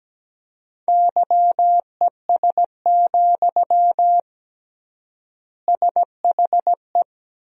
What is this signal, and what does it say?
Morse code, heard as a keyed tone, about 17 wpm, 700 Hz. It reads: YES, SHE
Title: 3x Word-Spacing - Sentences from Top 500 Words - 17wpm